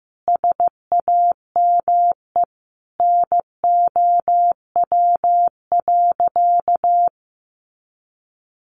Morse code, heard as a keyed tone, about 15 wpm, 700 Hz. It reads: SAME NOW.
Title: Sentences from Top 600 Words 15wpm